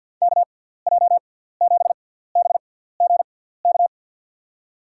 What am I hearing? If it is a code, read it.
Morse code, heard as a keyed tone, about 52 wpm, spaced as for 17 wpm, 700 Hz. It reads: KJ7BGK
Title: USA Call Signs - 6 Characters - 6x Character Spacing - 50wpm